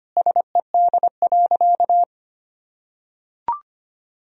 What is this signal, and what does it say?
Morse code, heard as a keyed tone, about 25 wpm, 700 Hz. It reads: SED.
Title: Sentences from Top 400 Words and Abbreviations 25wpm